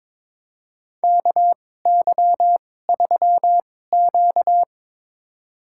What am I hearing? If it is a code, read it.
Morse code, heard as a keyed tone, about 22 wpm, spaced as for 16 wpm, 700 Hz. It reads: KY3Q
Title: USA Call Signs - 2x Character Spacing - 22wpm